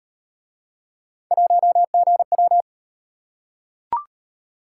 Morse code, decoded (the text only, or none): 1GW